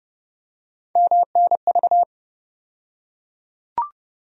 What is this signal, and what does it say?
Morse code, heard as a keyed tone, about 32 wpm, 700 Hz. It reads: MNV
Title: USA Call Sign Postfixes - 3 Letters 30wpm